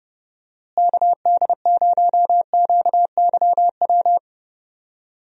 Morse code, heard as a keyed tone, about 31 wpm, 700 Hz. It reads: KD0QYW
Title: USA Call Signs - 6 Characters - Repeated 2x - 30wpm